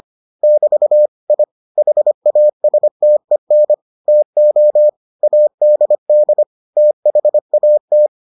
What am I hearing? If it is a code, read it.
Morse code, heard as a keyed tone, about 25 wpm, 600 Hz. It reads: <BT> I HASTEN TO ADD THAT